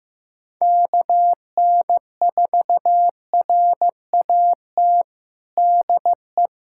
Morse code, heard as a keyed tone, about 15 wpm, 700 Hz. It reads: KN4RAT DE
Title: USA Callsign DE Callsign - 6 Characters - 15wpm